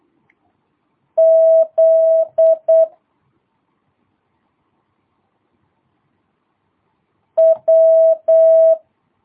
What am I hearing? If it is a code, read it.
Morse code, heard as a keyed tone, about 8 wpm, 650 Hz. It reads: Z W
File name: jh8cbh / 2022cw